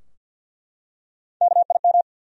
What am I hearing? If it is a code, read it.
Morse code, heard as a keyed tone, about 47 wpm, 700 Hz. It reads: KIM
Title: USA Common Names - 50wpm